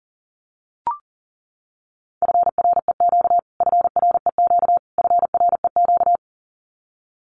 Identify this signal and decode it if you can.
Morse code, heard as a keyed tone, about 40 wpm, 700 Hz. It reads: FREQ FREQ FREQ
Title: Abbreviations Repeated 3x - 40wpm